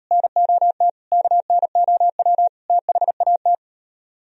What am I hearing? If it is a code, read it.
Morse code, heard as a keyed tone, about 38 wpm, 700 Hz. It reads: NOT KNOW THAT